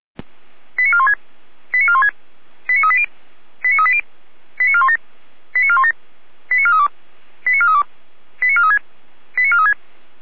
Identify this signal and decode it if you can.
Five-tone selective call (ZVEI), heard as quick runs of five tones, tones of about 70 ms. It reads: 89416 89416 89380 89380 89417 89417 89432 89432 89436 89436